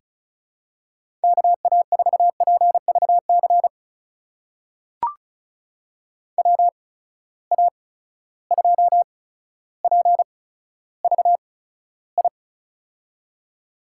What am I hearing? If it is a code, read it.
Morse code, heard as a keyed tone, about 34 wpm, spaced as for 15 wpm, 700 Hz. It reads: KA4PVC WA2PVI
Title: USA Call Signs - 6 Characters - 8x Character Spacing - 35wpm